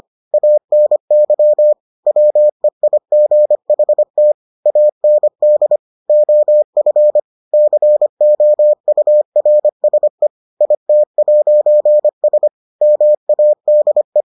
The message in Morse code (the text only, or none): ANY WEIGHT AND OF COURSE IT'S MADE